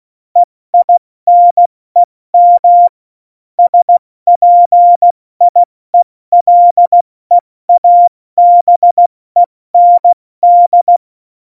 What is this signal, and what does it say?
Morse code, heard as a keyed tone, about 16 wpm, 700 Hz. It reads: EINEM SPIELEABEND
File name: Morsekurs / Witze / Witz 20. Oktober 2020, 70 F80